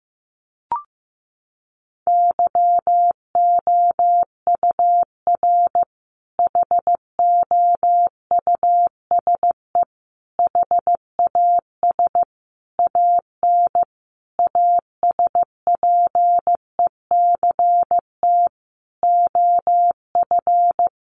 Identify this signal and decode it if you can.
Morse code, heard as a keyed tone, about 14 wpm, 700 Hz. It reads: YOUR HOUSE HAS AN ASPECT OF
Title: Sentences from Top 1300 Words 15wpm